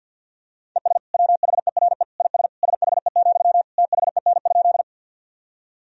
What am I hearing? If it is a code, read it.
Morse code, heard as a keyed tone, about 52 wpm, 700 Hz. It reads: ES WHERE IS SHE, THEN?
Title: Sentences from Top 200 Words and Abbreviations 50wpm